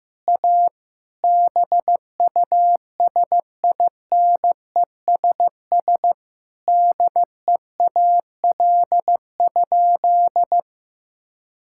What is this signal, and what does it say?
Morse code, heard as a keyed tone, about 16 wpm, 700 Hz. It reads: A BUSINESS DEAL?
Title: Sentences from Top 1000 Words - Review Part 6 - 15wpm